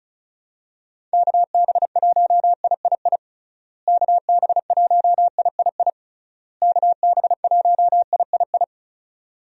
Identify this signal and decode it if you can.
Morse code, heard as a keyed tone, about 36 wpm, 700 Hz. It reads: KB1III KB1III KB1III
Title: USA Call Signs - 6 Characters - Repeated 3x - 35wpm